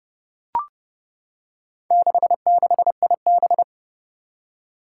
Morse code, heard as a keyed tone, about 30 wpm, 700 Hz. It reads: 66IB
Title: Four Challenging Letter-Number Combinations 30wpm